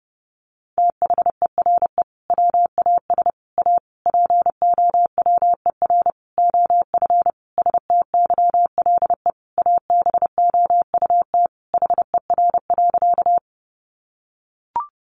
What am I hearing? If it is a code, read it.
Morse code, heard as a keyed tone, about 30 wpm, 700 Hz. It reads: THERE WAS A POWER OF STYLE ABOUT HER.
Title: Sentences from Top 800 Words 30wpm